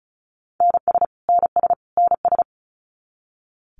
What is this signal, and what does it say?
Morse code, heard as a keyed tone, about 34 wpm, 700 Hz. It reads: NS NS NS
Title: Canadian Provinces and Territories - Repeated 3x - 35wpm